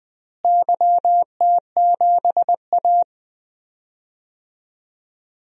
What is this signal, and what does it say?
Morse code, heard as a keyed tone, about 20 wpm, 700 Hz. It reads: YT7A
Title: DXCC-100 Call Signs - 4 Characters - Repeated 3x - 20wpm